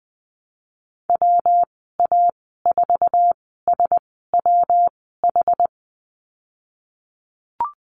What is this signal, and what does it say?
Morse code, heard as a keyed tone, about 20 wpm, spaced as for 15 wpm, 700 Hz. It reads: WA4SWH
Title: 122 Lesson 20g - Number 3 - Call Signs 20wpm